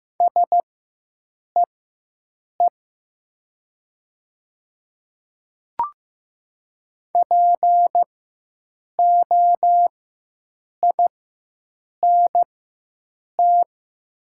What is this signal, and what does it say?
Morse code, heard as a keyed tone, about 15 wpm, spaced as for 7 wpm, 700 Hz. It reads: SEE POINT
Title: 4x Character Spacing - Top 300 Words - 15wpm